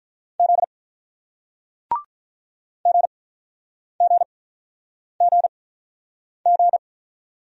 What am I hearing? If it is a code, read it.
Morse code, heard as a keyed tone, about 47 wpm, spaced as for 9 wpm, 700 Hz. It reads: Z GGGG